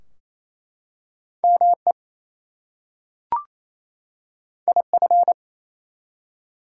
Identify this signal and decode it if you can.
Morse code, heard as a keyed tone, about 27 wpm, 700 Hz. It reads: ME IF
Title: Two Letter Words - 28wpm